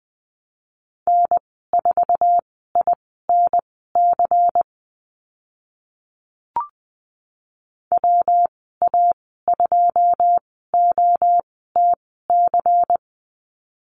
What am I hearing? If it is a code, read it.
Morse code, heard as a keyed tone, about 20 wpm, spaced as for 15 wpm, 700 Hz. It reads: N4INC WA2OTC
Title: 140 Lesson 23g - Letter f - Call Signs 20wpm